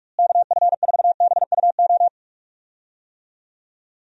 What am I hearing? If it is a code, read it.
Morse code, heard as a keyed tone, about 46 wpm, 700 Hz. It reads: KF4BUO
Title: USA Call Signs - 6 Characters - Repeated 3x - 45wpm